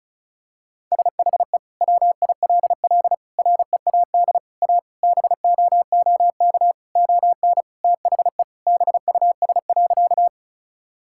Morse code, heard as a keyed tone, about 36 wpm, 700 Hz. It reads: SHE WILL READ A BOOK ON THE BUS.
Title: Sentences from Top 1000 Words - Review Part 6 - 35wpm